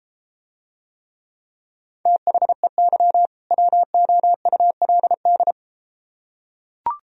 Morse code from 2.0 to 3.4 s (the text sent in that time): THEY